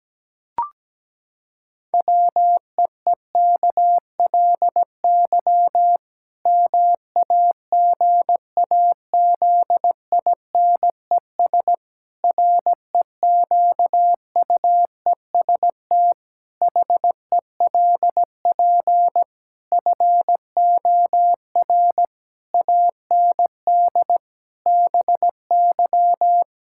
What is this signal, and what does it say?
Morse code, heard as a keyed tone, about 17 wpm, 700 Hz. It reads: WEEKLY MAGAZINES REQUEST HELP FOR AND BY
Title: Pangram Challenge v2 - 17wpm